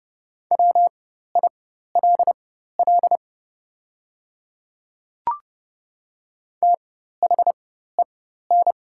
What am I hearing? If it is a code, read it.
Morse code, heard as a keyed tone, about 30 wpm, spaced as for 14 wpm, 700 Hz. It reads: WILL THEN
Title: 4x Character Spacing - Top 200 Words - 30wpm